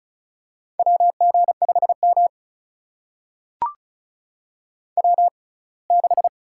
Morse code, heard as a keyed tone, about 36 wpm, spaced as for 24 wpm, 700 Hz. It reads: WG5M W6
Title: USA Call Signs - 4 Characters - 6x Character Spacing - 35wpm